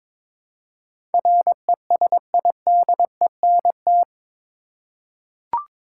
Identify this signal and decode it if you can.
Morse code, heard as a keyed tone, about 22 wpm, 700 Hz. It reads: RESIDENT
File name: practice / 22wpm / Eight Letter Words - 22wpm